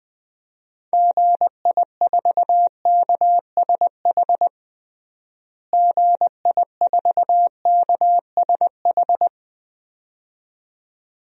Morse code, heard as a keyed tone, about 21 wpm, 700 Hz. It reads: GI4KSH GI4KSH
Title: DXCC-100 Call Signs - 6 Characters - Repeated 2x - 20wpm